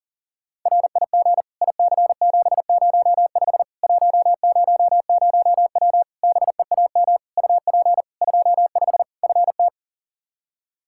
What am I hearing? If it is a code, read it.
Morse code, heard as a keyed tone, about 40 wpm, 700 Hz. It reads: RIG IC705 100W BEAM UP 25 FT
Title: K1USN SST QSO Practice 40wpm